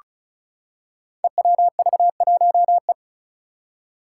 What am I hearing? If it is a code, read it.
Morse code, heard as a keyed tone, about 36 wpm, 700 Hz. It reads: EWV1E
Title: Five Challenging Letter-Number Combinations 35wpm